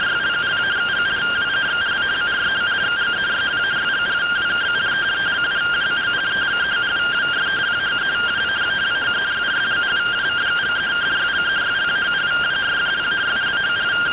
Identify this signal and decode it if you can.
Two-tone teleprinter signal, about 170 Hz shift.
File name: findings / 81 axial iodine 262.gen-17 rtty rx